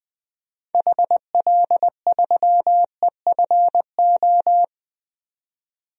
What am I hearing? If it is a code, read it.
Morse code, heard as a keyed tone, about 21 wpm, 700 Hz. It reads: HL3EFO
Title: DXCC-100 Call Signs - 6 Characters - 20wpm